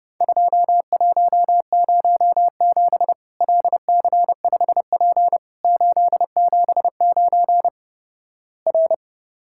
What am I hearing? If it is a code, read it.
Morse code, heard as a keyed tone, about 30 wpm, 700 Hz. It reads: 2107 LC5P 879 R
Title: Worked All Europe DX Contest - QTC Exchange - 30wpm